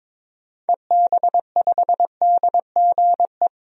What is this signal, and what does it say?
Morse code, heard as a keyed tone, about 22 wpm, 700 Hz. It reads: EB5DGE
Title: DXCC-100 Call Signs - 6 Characters - 22wpm